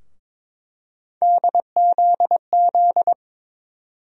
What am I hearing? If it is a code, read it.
Morse code, heard as a keyed tone, about 22 wpm, 700 Hz. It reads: DZZ
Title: USA Call Sign Postfixes - 3 Letters 22wpm